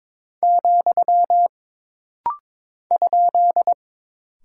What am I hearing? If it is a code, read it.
Morse code, heard as a keyed tone, about 22 wpm, 700 Hz. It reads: , ?
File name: practice / 22wpm / Single Punctuation-Prosign - Rapid-Fire - 22wpm